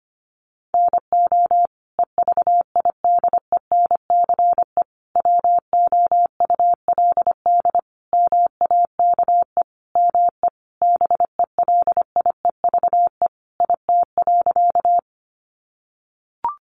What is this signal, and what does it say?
Morse code, heard as a keyed tone, about 25 wpm, 700 Hz. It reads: NO EVIDENCE WOULD MAKE ME BELIEVE IT.
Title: Sentences from Top 1300 Words 25wpm